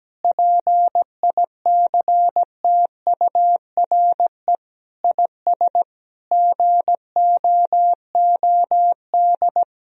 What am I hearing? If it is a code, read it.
Morse code, heard as a keyed tone, about 17 wpm, 700 Hz. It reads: PICTURE IS GOOD
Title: Sentences from Top 1000 Words - Review Part 1 - 17wpm